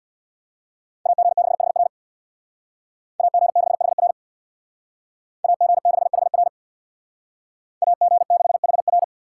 Morse code, heard as a keyed tone, about 63 wpm, 700 Hz. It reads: AG6HR AG6HR AG6HR AG6HR